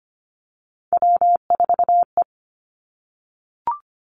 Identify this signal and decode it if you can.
Morse code, heard as a keyed tone, about 25 wpm, 700 Hz. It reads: W4E
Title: Three Challenging Letter-Number Combinations 25wpm